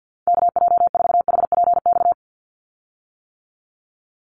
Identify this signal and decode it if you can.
Morse code, heard as a keyed tone, about 50 wpm, 700 Hz. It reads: KJ4HPX
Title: USA Call Signs 50wpm